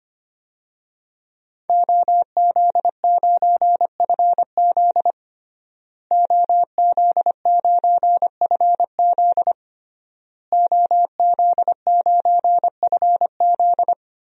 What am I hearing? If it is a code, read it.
Morse code, heard as a keyed tone, about 25 wpm, 700 Hz. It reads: OZ9FZ OZ9FZ OZ9FZ
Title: DXCC-100 Call Signs - 5 Characters - Repeated 3x - 25wpm